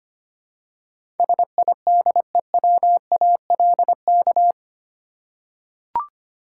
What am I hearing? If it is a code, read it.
Morse code, heard as a keyed tone, about 25 wpm, 700 Hz. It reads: SIDEWALK
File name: 8 Letter Words - First 3 Letters Given - 25wpm